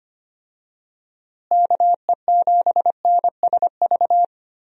Morse code, heard as a keyed tone, about 25 wpm, 700 Hz. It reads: KE7NSV
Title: USA Call Signs - 6 Characters - Repeated 2x - 25wpm